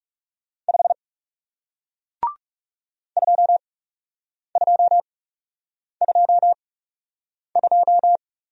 Morse code, heard as a keyed tone, about 41 wpm, spaced as for 8 wpm, 700 Hz. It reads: 5 2222